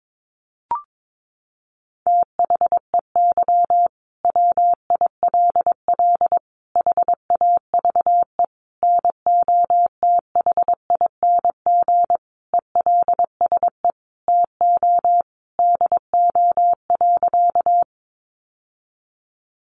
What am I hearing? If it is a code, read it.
Morse code, heard as a keyed tone, about 22 wpm, 700 Hz. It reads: THEY WILL HAVE NOTHING ELSE TO DO.
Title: Sentences from Top 800 Words 22wpm